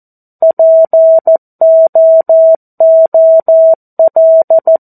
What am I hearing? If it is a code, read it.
Morse code, heard as a keyed tone, about 14 wpm, 650 Hz. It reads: POOL
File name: 02-Chapter-02 14 WPM